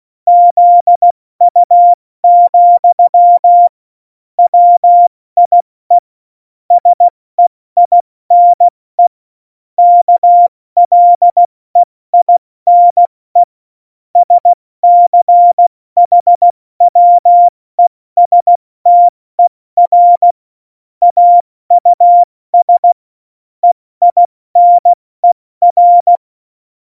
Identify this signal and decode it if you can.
Morse code, heard as a keyed tone, about 16 wpm, 700 Hz. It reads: ZU, WIE SEINE KLEINE SCHWESTER AUS EINER
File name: Morsekurs / Witze / Witz 29. August 17, Tempo 70 F80